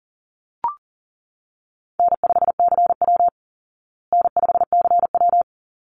Morse code, heard as a keyed tone, about 40 wpm, 700 Hz. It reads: N5CW N5CW
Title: Call Signs Repeated 3x - 40wpm